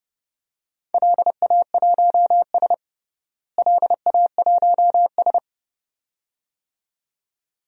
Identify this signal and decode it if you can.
Morse code, heard as a keyed tone, about 30 wpm, 700 Hz. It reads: LA1S LA1S
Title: DXCC-100 Call Signs - 4 Characters - Repeated 2x - 30wpm